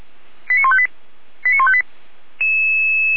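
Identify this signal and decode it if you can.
Five-tone selective call (ZVEI), heard as quick runs of five tones, tones of about 70 ms. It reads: 89158 89158
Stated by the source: ZVEI five-tone selective call received by SDR.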